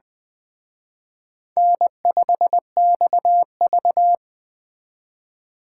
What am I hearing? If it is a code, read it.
Morse code, heard as a keyed tone, about 20 wpm, 700 Hz. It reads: N5XV